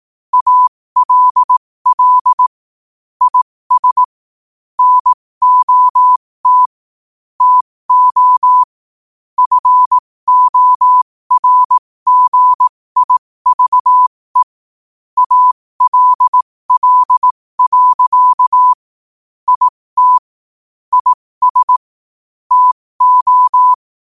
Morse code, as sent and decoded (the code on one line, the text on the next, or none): .- .-.. .-.. / .. ... / -. --- - / - --- / ..-. --- .-. --. .. ...- . / .- .-.. .-.. .-.-.- / .. - / .. ... / - ---
ALL IS NOT TO FORGIVE ALL. IT IS TO